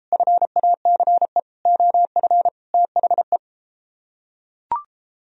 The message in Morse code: FACE OF THE

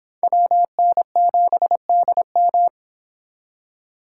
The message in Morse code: WN7DM

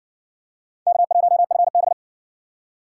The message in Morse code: K1FB

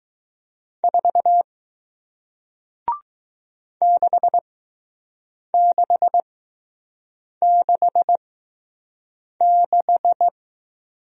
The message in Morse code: 4 6666

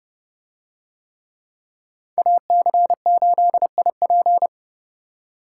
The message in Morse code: AC8IP

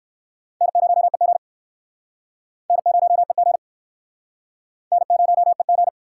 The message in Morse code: N0EG N0EG N0EG

